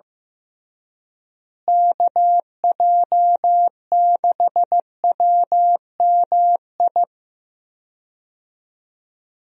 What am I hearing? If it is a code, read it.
Morse code, heard as a keyed tone, about 15 wpm, 700 Hz. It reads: KJ6WMI
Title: USA Call Signs - 6 Characters - Repeated 3x - 15wpm